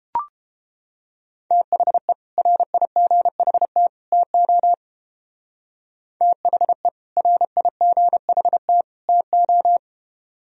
Morse code, heard as a keyed tone, about 28 wpm, 700 Hz. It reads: THE RIGHT TO THE RIGHT TO